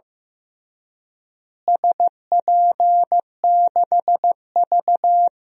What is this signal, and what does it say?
Morse code, heard as a keyed tone, about 15 wpm, 700 Hz. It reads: SP6V
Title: DXCC-100 Call Signs - 4 Characters - Repeated 3x - 15wpm